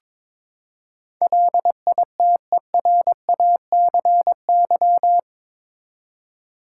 Morse code, heard as a keyed tone, about 21 wpm, 700 Hz. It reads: LITERACY
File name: practice / 22wpm / Eight Letter Words - 22wpm